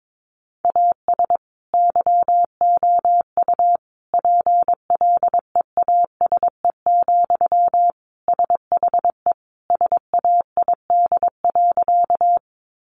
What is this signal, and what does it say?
Morse code, heard as a keyed tone, about 22 wpm, 700 Hz. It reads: AS YOU PLEASE, SHE SAID.
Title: Sentences from Top 1500 Words 22wpm